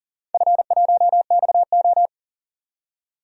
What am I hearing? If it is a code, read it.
Morse code, heard as a keyed tone, about 38 wpm, 700 Hz. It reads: F1XO
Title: DXCC-100 Call Signs - 4 Characters - 40wpm